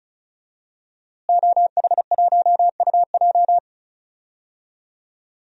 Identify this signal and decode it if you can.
Morse code, heard as a keyed tone, about 36 wpm, 700 Hz. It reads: OH1UJ
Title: DXCC-100 Call Signs - 5 Characters - Repeated 3x - 35wpm